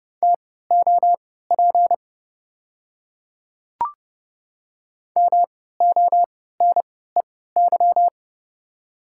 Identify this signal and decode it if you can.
Morse code, heard as a keyed tone, about 32 wpm, spaced as for 17 wpm, 700 Hz. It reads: TOP MONEY
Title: 3x Character Spacing - Top 500 Words - 30wpm